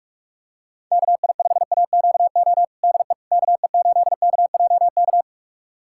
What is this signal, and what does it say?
Morse code, heard as a keyed tone, about 42 wpm, 700 Hz. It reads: KI5AQO DE KE8KJK
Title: USA Callsign DE Callsign 45wpm